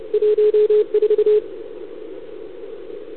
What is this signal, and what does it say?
Morse code, heard as a keyed tone, about 34 wpm, 440 Hz. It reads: 14
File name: CQWWCW2021 / UN9L